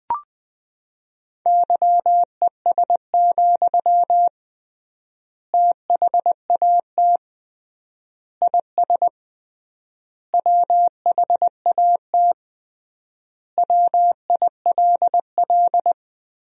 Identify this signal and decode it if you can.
Morse code, heard as a keyed tone, about 20 wpm, 700 Hz. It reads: YES, THAT IS WHAT WILL
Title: 3x Word-Spacing - Sentences from Top 500 Words - 20wpm